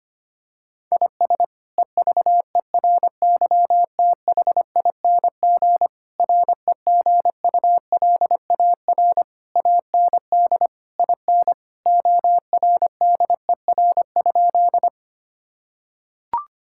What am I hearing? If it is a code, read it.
Morse code, heard as a keyed tone, about 25 wpm, 700 Hz. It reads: IS EVERYTHING REGULAR AND IN ORDER?
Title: Sentences from Top 1000 Words - Review Part 6 - 25wpm